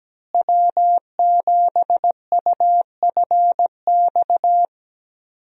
Morse code, heard as a keyed tone, about 17 wpm, 700 Hz. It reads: W7UFX
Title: USA Call Signs - 5 Characters - Repeated 2x - 17wpm